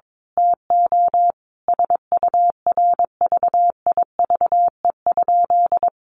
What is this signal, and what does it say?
Morse code, heard as a keyed tone, about 22 wpm, 700 Hz. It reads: TO SURVIVE?
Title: A Sound Of Thunder 22wpm